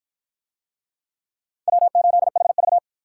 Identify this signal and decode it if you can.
Morse code, heard as a keyed tone, about 52 wpm, 700 Hz. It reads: W8HV